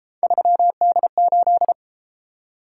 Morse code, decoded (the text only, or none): 3D8